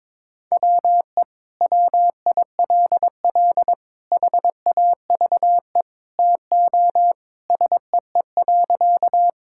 Morse code, heard as a keyed tone, about 21 wpm, 700 Hz. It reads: WE WILL HAVE TO SEE.
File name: practice / 22wpm / Sentences with 3-Word Phrases - 22wpm